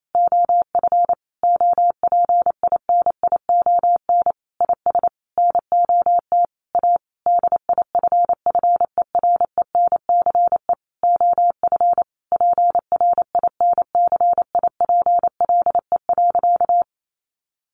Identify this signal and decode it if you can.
Morse code, heard as a keyed tone, about 28 wpm, 700 Hz. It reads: OF OPINION IS NOT A DIFFERENCE OF PRINCIPLE.
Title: Sentences from Top 1400 Words 28wpm